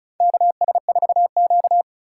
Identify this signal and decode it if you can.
Morse code, heard as a keyed tone, about 34 wpm, 700 Hz. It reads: KS4Q
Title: USA Call Signs - 3x Character Spacing - 35wpm